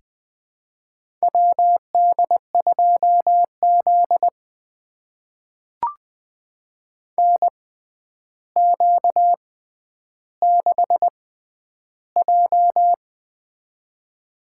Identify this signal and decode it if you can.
Morse code, heard as a keyed tone, about 20 wpm, spaced as for 10 wpm, 700 Hz. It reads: WD2Z NQ6J
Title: USA Call Signs - 4 Characters - 6x Character Spacing - 20wpm